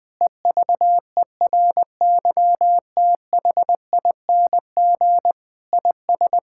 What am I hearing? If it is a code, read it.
Morse code, heard as a keyed tone, about 20 wpm, 700 Hz. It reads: EVERYTHING IS
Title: Sentences with 6-Word Phrases - 20wpm